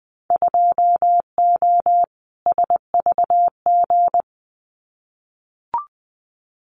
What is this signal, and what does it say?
Morse code, heard as a keyed tone, about 20 wpm, 700 Hz. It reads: 2O SVG
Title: Winter Field Day Exchanges 20wpm